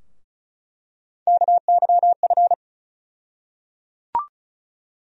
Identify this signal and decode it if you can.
Morse code, heard as a keyed tone, about 36 wpm, 700 Hz. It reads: KYF
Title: USA Call Sign Postfixes - 3 Letters 35wpm